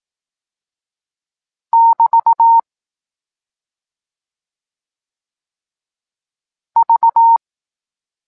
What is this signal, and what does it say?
Morse code, heard as a keyed tone, about 18 wpm, 920 Hz. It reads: <BT> V
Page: Tekst 14 – doskonalenie odbioru poznanych liter